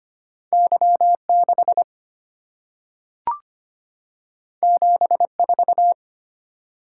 Morse code, heard as a keyed tone, about 25 wpm, 700 Hz. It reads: Y6 74
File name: Two Letter-Number Combinations - 25wpm